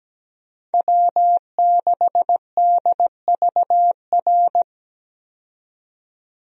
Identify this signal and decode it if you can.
Morse code, heard as a keyed tone, about 17 wpm, 700 Hz. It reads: W6DVR
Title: USA Call Signs - 5 Characters - 17wpm